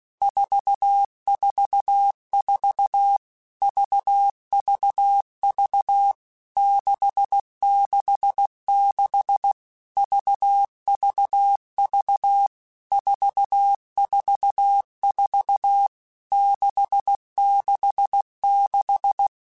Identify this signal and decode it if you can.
Morse code, heard as a keyed tone, about 16 wpm, 800 Hz. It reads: 444 VVV 666 VVV 444 666
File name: Basic Letter Groups By7Groups / VB46 / VB46@15